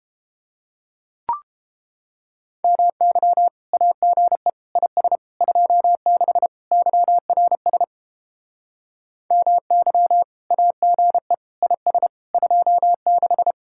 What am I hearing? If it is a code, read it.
Morse code, heard as a keyed tone, about 29 wpm, 700 Hz. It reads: MY AGE IS 26 YRS MY AGE IS 26